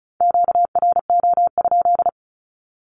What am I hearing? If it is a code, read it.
Morse code, heard as a keyed tone, about 36 wpm, 700 Hz. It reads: QRO?
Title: Q-Codes with Spoken Meaning - 35wpm